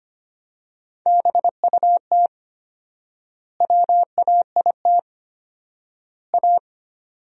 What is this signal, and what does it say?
Morse code, heard as a keyed tone, about 25 wpm, 700 Hz. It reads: BUT WAIT A
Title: 4x Word-Spacing - Sentences from Top 500 Words - 25wpm